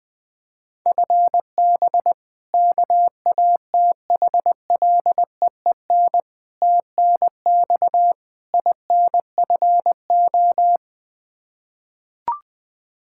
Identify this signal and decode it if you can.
Morse code, heard as a keyed tone, about 21 wpm, 700 Hz. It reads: FB KATHLEEN TNX INFO
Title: Practice QSO - Exchange 3 - More Info - 20wpm